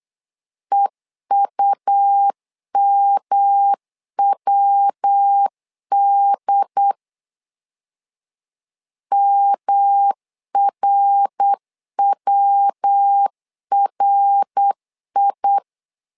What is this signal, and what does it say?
Morse code, heard as a keyed tone, about 8 wpm, 800 Hz. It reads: EUMWD MRWRI
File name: LEZIONI / 3 MWRO / 3 MWR 8wpm 2444